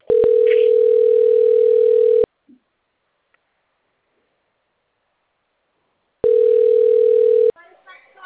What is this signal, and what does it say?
Ringback tone (North America), 440 + 480 Hz, with speech.
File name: out- - - - - .83318